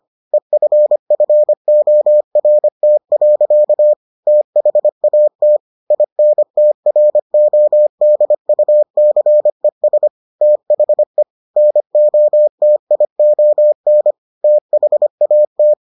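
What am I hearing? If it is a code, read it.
Morse code, heard as a keyed tone, about 25 wpm, 600 Hz. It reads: EFFORT. THAT INTRODUCES THE NOTION THAT
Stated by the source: Morse edition of the Foundations of Amateur Radio episodes.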